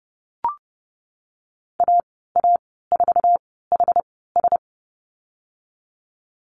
Morse code, heard as a keyed tone, about 30 wpm, spaced as for 17 wpm, 700 Hz. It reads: AA4HS
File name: Lesson 19g - Letter w - Call Signs 30wpm